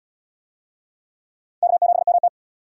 Morse code, heard as a keyed tone, about 77 wpm, 700 Hz. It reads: K7MT